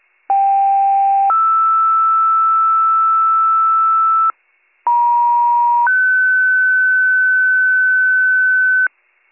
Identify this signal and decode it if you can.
Two-tone page, heard as a one-second tone, then a three-second tone, about 790 then 1320 Hz, about 950 then 1500 Hz, with noise. It is heard from 0.3 to 8.9 s.